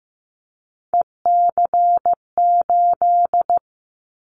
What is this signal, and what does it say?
Morse code, heard as a keyed tone, about 15 wpm, 700 Hz. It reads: EC8
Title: Callsign Prefixes for Top 100 DXCC - 15wpm